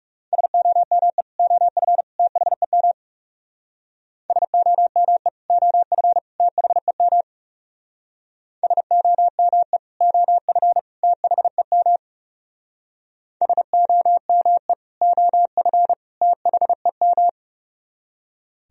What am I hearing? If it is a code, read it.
Morse code, heard as a keyed tone, about 37 wpm, 700 Hz. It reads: SOME OF THEM SOME OF THEM SOME OF THEM SOME OF THEM
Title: Speed-Racing - Sets of 3 Words 30wpm